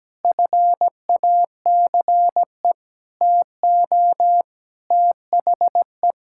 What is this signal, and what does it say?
Morse code, heard as a keyed tone, about 17 wpm, 700 Hz. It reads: FACE TO THE
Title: Sentences from Top 1000 Words - Review Part 6 - 17wpm